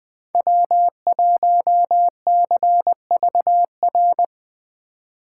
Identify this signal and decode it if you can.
Morse code, heard as a keyed tone, about 20 wpm, 700 Hz. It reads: W1CVR
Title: USA Call Signs - 5 Characters - 4x Character Spacing - 20wpm